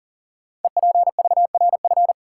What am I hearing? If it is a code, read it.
Morse code, heard as a keyed tone, about 38 wpm, 700 Hz. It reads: EPVRF
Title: Code Groups - 5 Letters - 40wpm